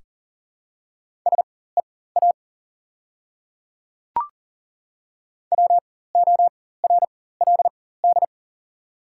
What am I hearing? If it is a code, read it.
Morse code, heard as a keyed tone, about 40 wpm, spaced as for 19 wpm, 690 Hz. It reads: SEA WORLD
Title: 4x Character Spacing - Top 300 Words - 40wpm